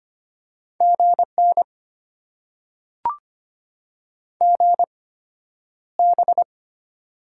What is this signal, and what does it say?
Morse code, heard as a keyed tone, about 25 wpm, spaced as for 11 wpm, 700 Hz. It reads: GN GB
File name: Top 90 QSO Elements - 8x Character Spacing - 25wpm